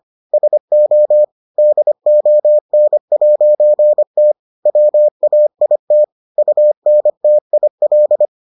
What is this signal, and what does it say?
Morse code, heard as a keyed tone, about 25 wpm, 600 Hz. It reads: SO DON'T WAIT UNTIL